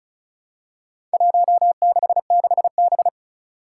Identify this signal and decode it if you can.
Morse code, heard as a keyed tone, about 36 wpm, 700 Hz. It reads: 166B
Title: Four Challenging Letter-Number Combinations 35wpm